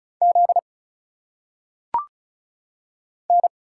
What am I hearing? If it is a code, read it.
Morse code, heard as a keyed tone, about 35 wpm, 700 Hz. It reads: Z N
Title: Single Letters - Phonetic Pronunciation 35wpm